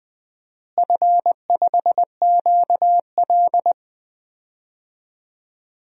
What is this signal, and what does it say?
Morse code, heard as a keyed tone, about 20 wpm, 700 Hz. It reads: F5QL
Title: DXCC-100 Call Signs - 4 Characters - Repeated 3x - 20wpm